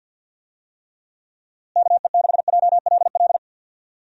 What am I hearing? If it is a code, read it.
Morse code, heard as a keyed tone, about 46 wpm, 700 Hz. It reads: KE6JLL
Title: USA Call Signs - 6 Characters - Repeated 3x - 50wpm